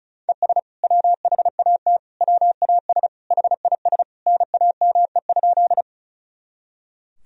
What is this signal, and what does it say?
Morse code, heard as a keyed tone, about 36 wpm, 700 Hz. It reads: ES WHAT WAS HIS NAME?
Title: Sentences from Top 200 Words and Abbreviations 35wpm